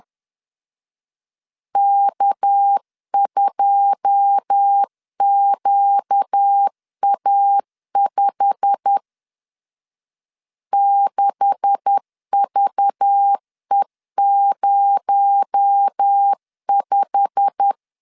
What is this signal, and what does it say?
Morse code, heard as a keyed tone, about 10 wpm, 800 Hz. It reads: K2QA5 6VE05
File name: CW / DET / 1 9 1500C S B 10wpm 4439